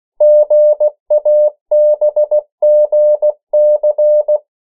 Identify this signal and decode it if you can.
Morse code, heard as a keyed tone, about 16 wpm, 600 Hz. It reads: GABGC
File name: morse mp3 / 14 urok 13